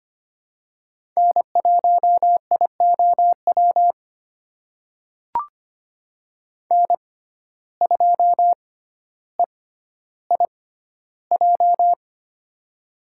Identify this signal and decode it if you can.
Morse code, heard as a keyed tone, about 25 wpm, spaced as for 13 wpm, 700 Hz. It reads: N1IOW N2EIJ
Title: USA Call Signs - 5 Characters - 6x Character Spacing - 25wpm